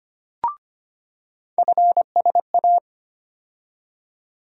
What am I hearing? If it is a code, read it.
Morse code, heard as a keyed tone, about 25 wpm, 700 Hz. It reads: FSA
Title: USA Call Sign Postfixes - 3 Letters 25wpm